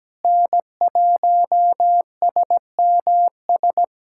Morse code, heard as a keyed tone, about 17 wpm, 700 Hz. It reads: N1SMS
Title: USA Callsign DE Callsign - 5 Characters - 17wpm